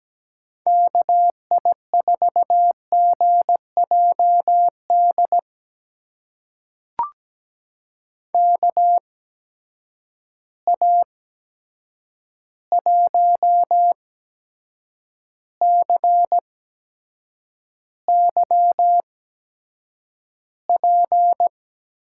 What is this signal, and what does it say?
Morse code, heard as a keyed tone, about 17 wpm, spaced as for 7 wpm, 700 Hz. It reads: KI4GJD KA1CYP
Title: USA Call Signs - 6 Characters - 8x Character Spacing - 17wpm